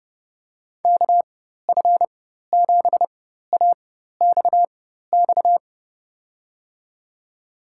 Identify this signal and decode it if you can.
Morse code, heard as a keyed tone, about 30 wpm, spaced as for 14 wpm, 700 Hz. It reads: KF7AXX
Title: USA Call Signs - 6 Characters - 4x Character Spacing - 30wpm